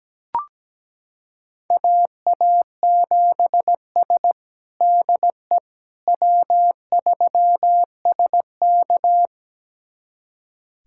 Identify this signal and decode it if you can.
Morse code, heard as a keyed tone, about 17 wpm, 700 Hz. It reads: AA7S DE W3SK
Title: USA Callsign DE Callsign - 4 Characters - 17wpm